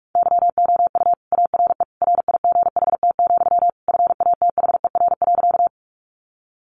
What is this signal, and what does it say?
Morse code, heard as a keyed tone, about 44 wpm, 700 Hz. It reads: YOU ARE RIGHT, FATHER.
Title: Sentences from Top 300 Words 45wpm